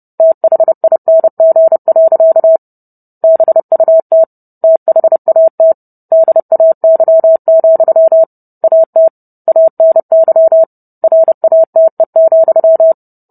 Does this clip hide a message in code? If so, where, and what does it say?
Yes: Morse code from 0.2 to 12.9 s, heard as a keyed tone, about 30 wpm, 650 Hz. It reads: THING. BUT THAT DAY, AT ANY RATE,